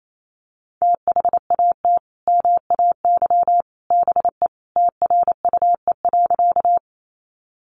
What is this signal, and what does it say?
Morse code, heard as a keyed tone, about 28 wpm, 700 Hz. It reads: THAT MAY BE TRUE.
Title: Sentences from Top 400 Words 28wpm